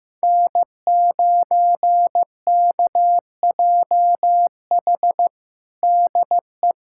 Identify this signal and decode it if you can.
Morse code, heard as a keyed tone, about 16 wpm, 700 Hz. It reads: N9KJH DE
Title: USA Callsign DE Callsign - 5 Characters - 15wpm